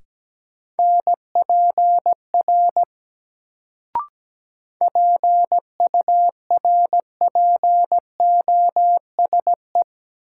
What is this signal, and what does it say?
Morse code, heard as a keyed tone, about 17 wpm, 700 Hz. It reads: NPR PURPOSE
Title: Sets of 5 Words - New Word Drill - 17wpm